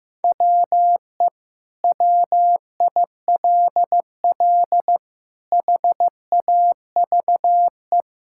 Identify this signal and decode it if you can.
Morse code, heard as a keyed tone, about 15 wpm, 700 Hz. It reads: WE WILL HAVE
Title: Sentences from Top 1000 Words - Review Part 5 - 15wpm